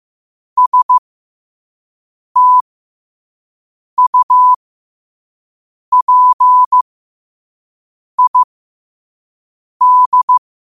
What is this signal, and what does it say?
Morse code, heard as a keyed tone, about 15 wpm, spaced as for 5 wpm, 1000 Hz. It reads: STUPID
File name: qotdfiles / 5wpm